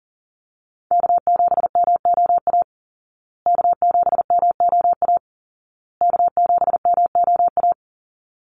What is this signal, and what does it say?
Morse code, heard as a keyed tone, about 40 wpm, 700 Hz. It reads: K7MOA K7MOA K7MOA
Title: Call Signs Repeated 2x - 40wpm